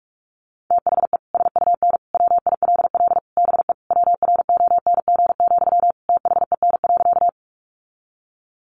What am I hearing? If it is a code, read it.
Morse code, heard as a keyed tone, about 45 wpm, 700 Hz. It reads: THE SUN WILL BE WRONG, THEN.
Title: Sentences from Top 900 Words 45wpm